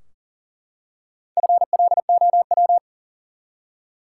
Morse code, heard as a keyed tone, about 40 wpm, 700 Hz. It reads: FLOW
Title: Four Letter Words - 40wpm